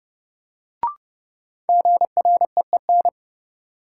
Morse code, heard as a keyed tone, about 32 wpm, 700 Hz. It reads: GREEN
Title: Sentences from Top 500 Words - New Word Drill - 30wpm